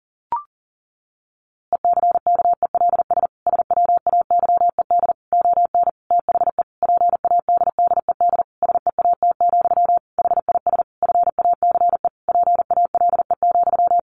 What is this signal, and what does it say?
Morse code, heard as a keyed tone, about 40 wpm, 700 Hz. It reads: ECKELS SWAYED ON THE PADDED SEAT, HIS FACE PALE,
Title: A Sound Of Thunder 40wpm